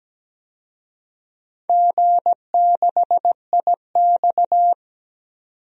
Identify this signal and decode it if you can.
Morse code, heard as a keyed tone, about 17 wpm, 700 Hz. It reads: G6IX
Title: DXCC-100 Call Signs - 4 Characters - 17wpm